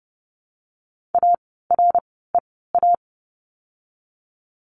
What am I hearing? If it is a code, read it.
Morse code, heard as a keyed tone, about 30 wpm, spaced as for 17 wpm, 700 Hz. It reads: AREA
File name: course / 30:10wpm / 108 Lesson 18f - Letter m - Words 30wpm